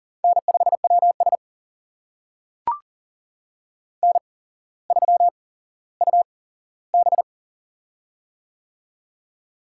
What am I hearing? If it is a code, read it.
Morse code, heard as a keyed tone, about 39 wpm, spaced as for 17 wpm, 700 Hz. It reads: N5WS N3UB
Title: USA Call Signs - 4 Characters - 8x Character Spacing - 40wpm